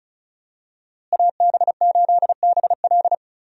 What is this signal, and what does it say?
Morse code, heard as a keyed tone, about 34 wpm, 700 Hz. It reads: AB8BL
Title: USA Call Signs - 5 Characters - Repeated 3x - 35wpm